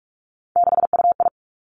Morse code, heard as a keyed tone, about 46 wpm, 700 Hz. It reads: 6UI
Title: Three Challenging Letter-Number Combinations 45wpm